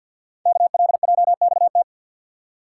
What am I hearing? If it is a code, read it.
Morse code, heard as a keyed tone, about 50 wpm, 700 Hz. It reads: KLJXT
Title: Code Groups - 5 Letter:Numbers - 50wpm